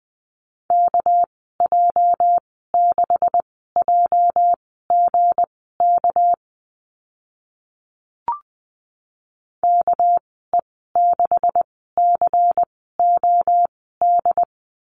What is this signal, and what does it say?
Morse code, heard as a keyed tone, about 20 wpm, spaced as for 14 wpm, 700 Hz. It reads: KJ6JGK KE6COD
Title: 207 Lesson 33g - Letter j - Call Signs 20wpm